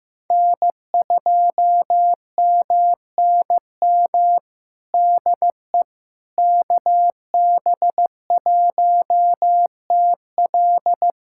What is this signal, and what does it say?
Morse code, heard as a keyed tone, about 15 wpm, 700 Hz. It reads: N2MNM DE KB1TL
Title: USA Callsign DE Callsign - 5 Characters - 15wpm